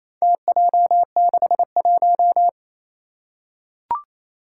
Morse code, heard as a keyed tone, about 28 wpm, 700 Hz. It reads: TJ61
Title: Four Challenging Letter-Number Combinations 28wpm